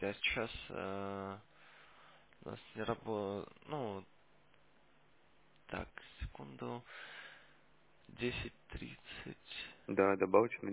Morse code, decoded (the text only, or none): none